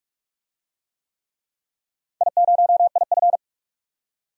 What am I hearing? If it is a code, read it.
Morse code, heard as a keyed tone, about 44 wpm, 700 Hz. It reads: I0IF
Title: DXCC-100 Call Signs - 4 Characters - 45wpm